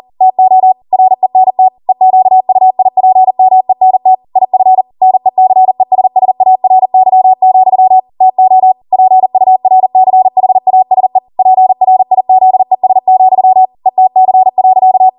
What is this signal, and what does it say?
Morse code, heard as a keyed tone, about 40 wpm, 750 Hz. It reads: TO RENT EQUIPMENT IF NECESSARY, TO PURCHASE PRIZES, ETC.